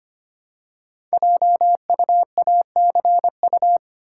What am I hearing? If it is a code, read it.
Morse code, heard as a keyed tone, about 25 wpm, 700 Hz. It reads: JUACU
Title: Code Groups - 5 Letters - 25wpm